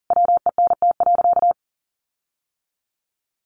Morse code, heard as a keyed tone, about 40 wpm, 700 Hz. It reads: WENT.
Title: Sentences from Top 1400 Words 40wpm